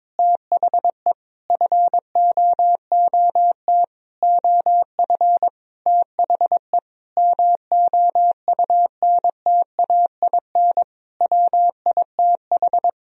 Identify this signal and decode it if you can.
Morse code, heard as a keyed tone, about 22 wpm, 690 Hz. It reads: THE FOOT OF THE MOUNTAIN WITH
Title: Sentences from Top 1000 Words - Review Part 3 - 22wpm